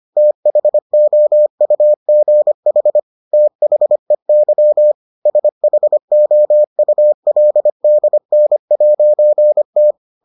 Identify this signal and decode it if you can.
Morse code, heard as a keyed tone, about 25 wpm, 600 Hz. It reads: THOUGH THEY SHOULDN'T